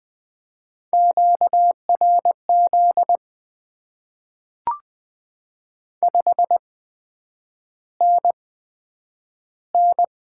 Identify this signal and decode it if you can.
Morse code, heard as a keyed tone, about 20 wpm, spaced as for 9 wpm, 700 Hz. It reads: QRZ 5NN
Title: Top 90 QSO Elements - 8x Character Spacing - 20wpm